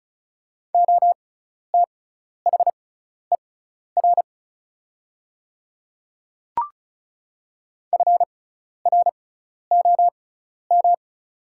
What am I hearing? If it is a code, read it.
Morse code, heard as a keyed tone, about 35 wpm, spaced as for 12 wpm, 690 Hz. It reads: OTHER FROM
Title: 6x Character Spacing - Top 200 Words - 35wpm